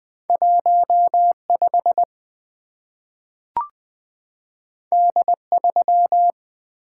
Morse code, heard as a keyed tone, about 20 wpm, 700 Hz. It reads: 15 D3
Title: Two Letter-Number Combinations - 20wpm